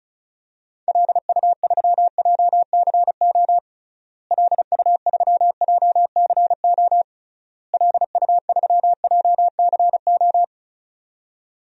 Morse code, heard as a keyed tone, about 36 wpm, 700 Hz. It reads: LU3JCO LU3JCO LU3JCO
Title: DXCC-100 Call Signs - 6 Characters - Repeated 3x - 35wpm